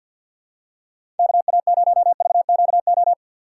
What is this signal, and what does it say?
Morse code, heard as a keyed tone, about 53 wpm, 690 Hz. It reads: KA0VQO
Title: USA Call Signs - 6x Character Spacing - 50wpm